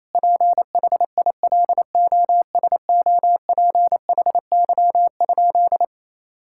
Morse code, heard as a keyed tone, about 28 wpm, 700 Hz. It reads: PHILOSOPHY?
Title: Sentences from Top 1000 Words - Review Part 6 - 28wpm